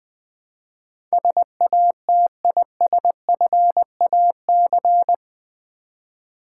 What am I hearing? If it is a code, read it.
Morse code, heard as a keyed tone, about 20 wpm, 700 Hz. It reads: SATISFAC
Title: Build A Word - 4 Letters At A Time - 12-Letter Words - 20wpm